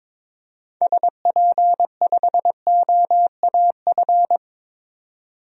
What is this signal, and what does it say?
Morse code, heard as a keyed tone, about 22 wpm, 700 Hz. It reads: SP5OAF
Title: DXCC-100 Call Signs - 6 Characters - 22wpm